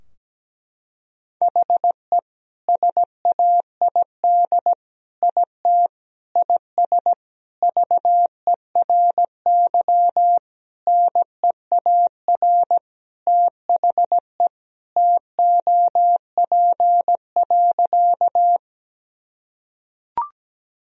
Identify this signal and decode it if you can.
Morse code, heard as a keyed tone, about 17 wpm, 700 Hz. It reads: HE SAID IT IS VERY NEAR THE TOP.
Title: Sentences from Top 1000 Words - Review Part 2 - 17wpm